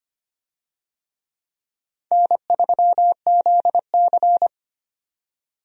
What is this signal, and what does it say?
Morse code, heard as a keyed tone, about 25 wpm, 700 Hz. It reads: N3ZC
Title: USA Call Signs - 4 Characters - 25wpm